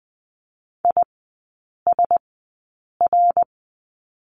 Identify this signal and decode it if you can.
Morse code, heard as a keyed tone, about 20 wpm, spaced as for 8 wpm, 700 Hz. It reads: ISR
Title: 053 Lesson 10e - Letter r - Three Characters 20wpm